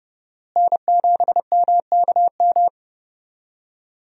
Morse code, heard as a keyed tone, about 30 wpm, 700 Hz. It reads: N7MKM
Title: USA Call Signs - 5 Characters - Repeated 2x - 30wpm